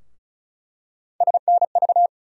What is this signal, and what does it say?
Morse code, heard as a keyed tone, about 36 wpm, 700 Hz. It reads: SNV
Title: USA Call Sign Postfixes - 3 Letters 35wpm